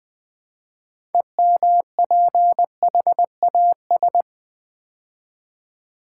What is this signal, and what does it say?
Morse code, heard as a keyed tone, about 20 wpm, 700 Hz. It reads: EMPHAS
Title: Build A Word - 3 Letters At A Time - 9-Letter Words - 20wpm